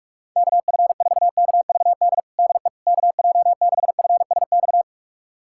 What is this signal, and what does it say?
Morse code, heard as a keyed tone, about 41 wpm, 700 Hz. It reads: KF4KVD DE KJ6FSX
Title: USA Callsign DE Callsign - 6 Characters - 45wpm